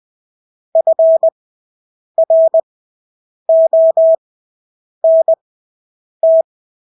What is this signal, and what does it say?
Morse code, heard as a keyed tone, about 20 wpm, spaced as for 8 wpm, 650 Hz. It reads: FRONT
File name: Engelske ord 20/8 wpm